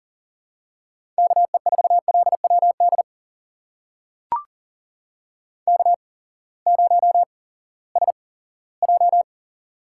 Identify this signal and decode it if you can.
Morse code, heard as a keyed tone, about 40 wpm, spaced as for 20 wpm, 700 Hz. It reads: KE4LWD K0SJ
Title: USA Call Signs - 8x Character Spacing - 40wpm